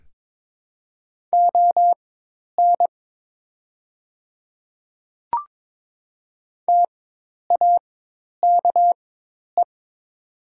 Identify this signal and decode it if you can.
Morse code, heard as a keyed tone, about 22 wpm, spaced as for 10 wpm, 700 Hz. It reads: ON TAKE
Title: 4x Character Spacing - Top 200 Words - 22wpm